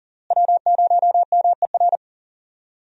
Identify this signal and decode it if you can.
Morse code, heard as a keyed tone, about 40 wpm, 700 Hz. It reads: W0MER
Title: USA Call Signs - 5 Characters - Repeated 3x - 40wpm